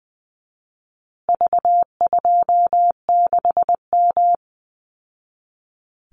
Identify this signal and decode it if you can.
Morse code, heard as a keyed tone, about 20 wpm, 700 Hz. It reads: V26M
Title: Call Signs Repeated 2x - 20wpm